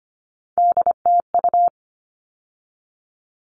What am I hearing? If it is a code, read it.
Morse code, heard as a keyed tone, about 25 wpm, 700 Hz. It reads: DTU
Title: All Cut Numbers - 3-Digit 25wpm